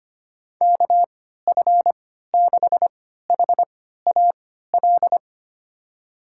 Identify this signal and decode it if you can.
Morse code, heard as a keyed tone, about 25 wpm, spaced as for 14 wpm, 700 Hz. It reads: KF6HAL
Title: USA Call Signs - 6 Characters - 3x Character Spacing - 25wpm